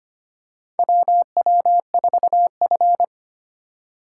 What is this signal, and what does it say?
Morse code, heard as a keyed tone, about 25 wpm, 700 Hz. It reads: WW4F
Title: USA Call Signs - 4x Character Spacing - 25wpm